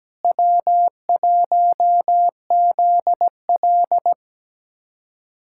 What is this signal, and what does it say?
Morse code, heard as a keyed tone, about 17 wpm, 700 Hz. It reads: W1ZL
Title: USA Callsign DE Callsign - 4 Characters - 17wpm